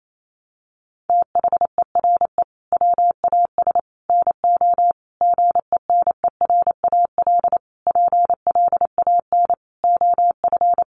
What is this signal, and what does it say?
Morse code, heard as a keyed tone, about 28 wpm, 700 Hz. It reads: THERE WAS NO GENERAL PLAN OF
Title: Sentences from Top 700 Words 28wpm